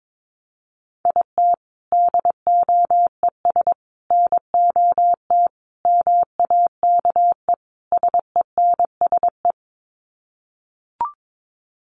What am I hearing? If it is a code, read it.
Morse code, heard as a keyed tone, about 22 wpm, 700 Hz. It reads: IT DOES NOT MAKE SENSE
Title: Sets of 5 Words Encore2 22wpm